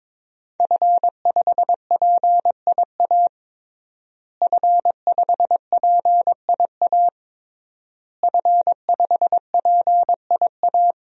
Morse code, heard as a keyed tone, about 22 wpm, 700 Hz. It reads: F5PIA F5PIA F5PIA
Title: DXCC-100 Call Signs - 5 Characters - Repeated 3x - 22wpm